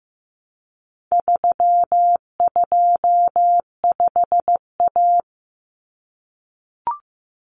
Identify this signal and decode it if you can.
Morse code, heard as a keyed tone, about 15 wpm, 700 Hz. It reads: 325A
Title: Common Cut Numbers - 4-Digit 15wpm